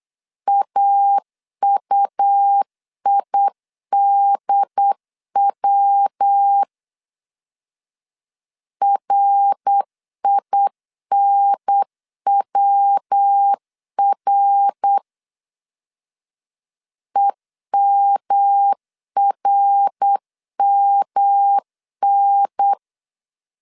Morse code, heard as a keyed tone, about 8 wpm, 800 Hz. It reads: AUIDW RINWR EMRMN
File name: CW / DET / 1 3 1500C S B 8wpm 4141